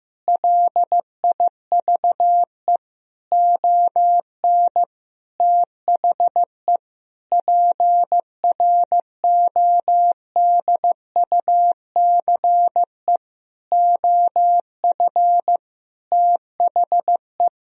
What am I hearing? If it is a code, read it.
Morse code, heard as a keyed tone, about 15 wpm, 700 Hz. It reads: LIVE ON THE PRODUCE OF THE
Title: Sentences from Top 1000 Words - Review Part 3 - 15wpm